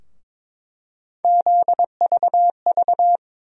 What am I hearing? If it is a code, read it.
Morse code, heard as a keyed tone, about 22 wpm, 700 Hz. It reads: ZVV
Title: USA Call Sign Postfixes - 3 Letters 22wpm